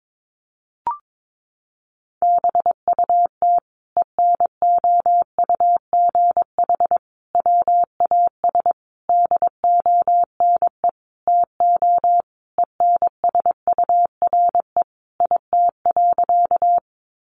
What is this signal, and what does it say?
Morse code, heard as a keyed tone, about 22 wpm, 700 Hz. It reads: BUT ENOUGH WAS DONE TO ENSURE IT.
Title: Sentences from Top 1100 Words 22wpm